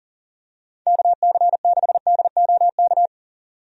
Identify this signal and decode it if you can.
Morse code, heard as a keyed tone, about 40 wpm, 700 Hz. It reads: KC6DOK
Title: USA Call Signs - 6 Characters - 40wpm